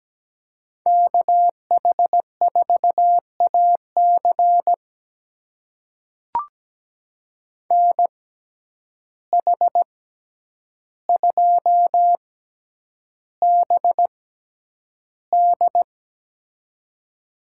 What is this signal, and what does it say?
Morse code, heard as a keyed tone, about 17 wpm, spaced as for 9 wpm, 700 Hz. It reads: KH4AC NH2BD
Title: USA Call Signs - 5 Characters - 6x Character Spacing - 17wpm